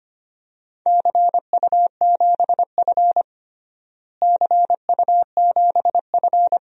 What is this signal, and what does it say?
Morse code, heard as a keyed tone, about 25 wpm, 700 Hz. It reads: CU7F CU7F
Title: DXCC-100 Call Signs - 4 Characters - Repeated 2x - 25wpm